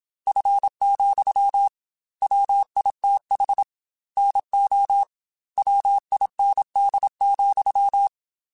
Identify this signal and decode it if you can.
Morse code, heard as a keyed tone, about 26 wpm, 800 Hz. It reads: F, WITH NO WIND,
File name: Weather Sets / Weather Sets-mp3s / Weather 3 17 @25